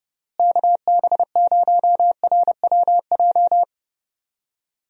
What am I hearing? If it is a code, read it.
Morse code, heard as a keyed tone, about 30 wpm, 700 Hz. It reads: KB0RWJ
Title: USA Call Signs - 6 Characters - Repeated 2x - 30wpm